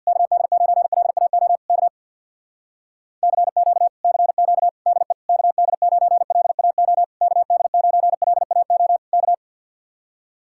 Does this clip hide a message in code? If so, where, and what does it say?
Yes: Morse code from 0.1 to 9.3 s, heard as a keyed tone, about 65 wpm, 700 Hz. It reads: KD9LAO K CQ CQ DE KD9LAO KD9LAO K